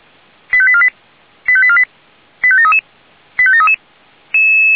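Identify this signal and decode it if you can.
Five-tone selective call (ZVEI), heard as quick runs of five tones, tones of about 70 ms. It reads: 85848 85848 85733 85733
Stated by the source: ZVEI five-tone call received by SDR.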